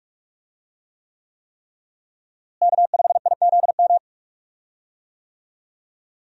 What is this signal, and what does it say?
Morse code, heard as a keyed tone, about 46 wpm, 700 Hz. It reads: K5IZM
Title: USA Call Signs - 5 Characters - Repeated 3x - 45wpm